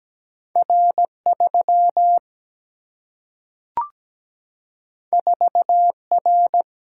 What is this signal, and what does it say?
Morse code, heard as a keyed tone, about 17 wpm, 700 Hz. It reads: R 3 4 R
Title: Single-Letter Single-Number 17wpm